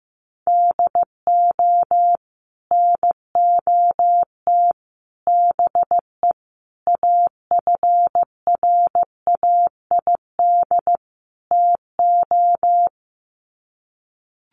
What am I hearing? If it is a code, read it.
Morse code, heard as a keyed tone, about 14 wpm, 700 Hz. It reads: DO NOT BE AFRAID TO
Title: Sets of 5 Words Encore2 15wpm